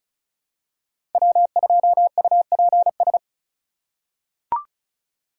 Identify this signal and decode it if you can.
Morse code, heard as a keyed tone, about 36 wpm, 700 Hz. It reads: W2UPS
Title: USA Call Signs - 4x Character Spacing - 35wpm